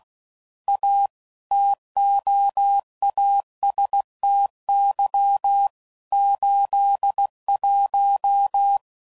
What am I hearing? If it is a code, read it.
Morse code, heard as a keyed tone, about 16 wpm, 800 Hz. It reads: A TOASTY 81
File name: CW Code Groups @15WPM / Weather Sets / Weather 5 12